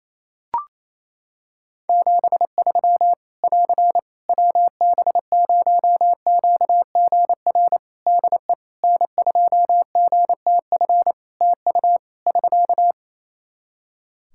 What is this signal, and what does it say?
Morse code, heard as a keyed tone, about 28 wpm, 700 Hz. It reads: 73 <AR> WB0QGR DE N2GTF TU <SK>
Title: Practice QSO - Exchange 4 - Ending - 28wpm